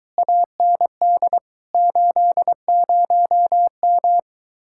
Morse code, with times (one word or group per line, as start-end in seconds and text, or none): 0.2-1.4 AND
1.7-4.2 80M